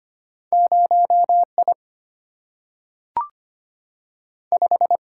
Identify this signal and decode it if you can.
Morse code, heard as a keyed tone, about 25 wpm, 700 Hz. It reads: 0I 5
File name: Two Letter-Number Combinations - 25wpm